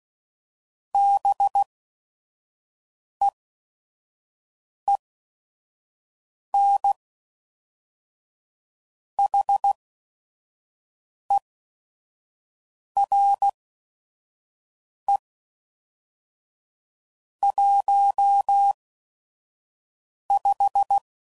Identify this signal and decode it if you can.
Morse code, heard as a keyed tone, about 16 wpm, spaced as for 5 wpm, 800 Hz. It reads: BEEN HERE 15
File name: CW Code Groups @5WPM / QSOs / WD0FBT 1